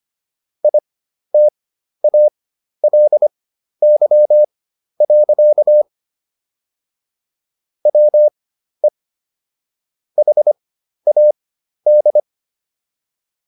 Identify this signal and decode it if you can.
Morse code, heard as a keyed tone, about 25 wpm, spaced as for 12 wpm, 600 Hz. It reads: ITALY. WE HAD